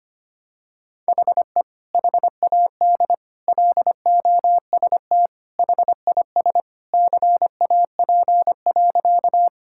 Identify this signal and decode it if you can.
Morse code, heard as a keyed tone, about 25 wpm, 700 Hz. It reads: HE HAD LOST HIS CAP.
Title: Sentences from Top 1600 Words 25wpm